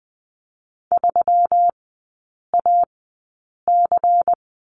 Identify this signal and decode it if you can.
Morse code, heard as a keyed tone, about 20 wpm, spaced as for 8 wpm, 700 Hz. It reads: 3AC
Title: 127 Lesson 21e - Number 6 - Three Characters 20wpm